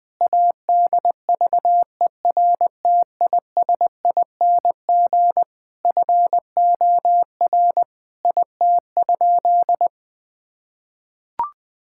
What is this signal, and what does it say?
Morse code, heard as a keyed tone, about 20 wpm, 700 Hz. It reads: ADVERTISING FOR IT?
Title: Sentences from Top 1000 Words - Review Part 6 - 20wpm